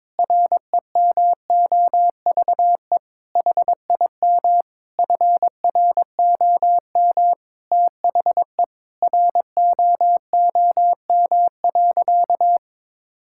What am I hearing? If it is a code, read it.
Morse code, heard as a keyed tone, about 22 wpm, 700 Hz. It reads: REMOVE HIM FROM THE ROOM.
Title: Sentences from Top 1000 Words - Review Part 5 - 22wpm